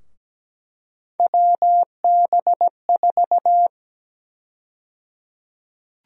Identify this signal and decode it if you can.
Morse code, heard as a keyed tone, about 17 wpm, 700 Hz. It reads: WB4
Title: USA Call Sign Prefixes - 17wpm